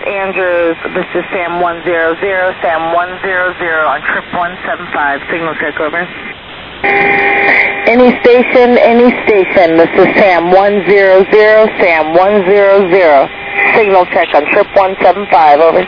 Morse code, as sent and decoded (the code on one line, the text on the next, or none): none